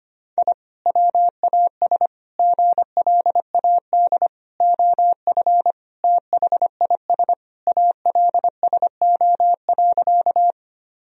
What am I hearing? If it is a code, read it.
Morse code, heard as a keyed tone, about 25 wpm, 700 Hz. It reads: I WAS GLAD OF THIS ALSO.
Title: Sentences from Top 1000 Words - Review Part 1 - 25wpm